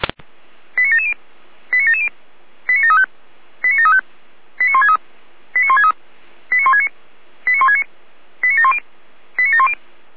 Five-tone selective call (ZVEI), heard as quick runs of five tones, tones of about 70 ms. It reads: 89770 89770 89735 89735 89163 89163 89169 89169 89710 89710